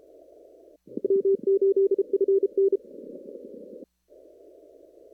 Morse code, heard as a keyed tone, about 32 wpm, 390 Hz. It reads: M8FN
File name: CW / kt5j wpx cw-op-k5pi